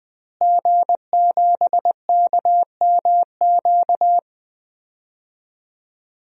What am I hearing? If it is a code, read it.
Morse code, heard as a keyed tone, about 21 wpm, 700 Hz. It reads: G7KMQ
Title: DXCC-100 Call Signs - 5 Characters - Repeated 2x - 20wpm